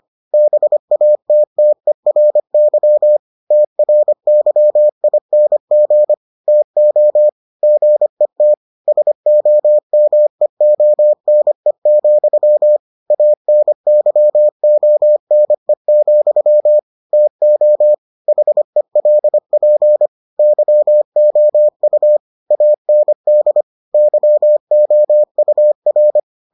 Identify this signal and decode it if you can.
Morse code, heard as a keyed tone, about 25 wpm, 600 Hz. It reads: BATTERY TRYING TO GET SOMEONE, ANYONE, TO HELP YOU AND YOUR